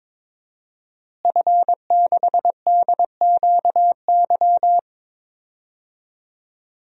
Morse code, heard as a keyed tone, about 22 wpm, 700 Hz. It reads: F6DQY
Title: DXCC-100 Call Signs - 5 Characters - Repeated 3x - 22wpm